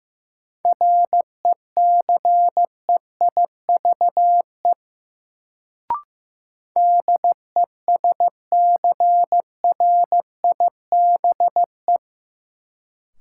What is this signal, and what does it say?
Morse code, heard as a keyed tone, about 15 wpm, 700 Hz. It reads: RECEIVE DESCRIBE
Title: Sentences from Top 1000 Words - New Word Drill - 15wpm